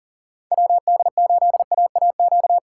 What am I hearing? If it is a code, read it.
Morse code, heard as a keyed tone, about 40 wpm, 700 Hz. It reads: WD8AAQ
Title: USA Call Signs - 6 Characters - 2x Character Spacing - 40wpm